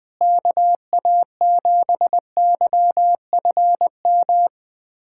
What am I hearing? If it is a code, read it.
Morse code, heard as a keyed tone, about 20 wpm, 700 Hz. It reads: KA7YFM
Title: USA Callsign DE Callsign - 6 Characters - 20wpm